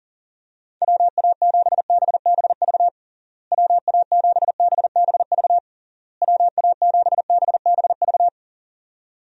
Morse code, heard as a keyed tone, about 40 wpm, 700 Hz. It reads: WA7BBV WA7BBV WA7BBV
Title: USA Call Signs - 6 Characters - Repeated 3x - 40wpm